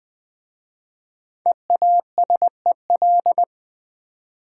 Morse code, heard as a keyed tone, about 20 wpm, 700 Hz. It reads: EASEL
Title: Easy Words with Hints - 20wpm